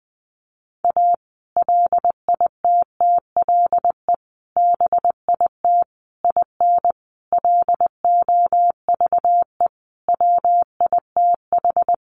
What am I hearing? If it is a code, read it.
Morse code, heard as a keyed tone, about 19 wpm, 700 Hz. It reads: A LITTLE BIT IN LOVE WITH
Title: Sentences from Top 900 Words 20wpm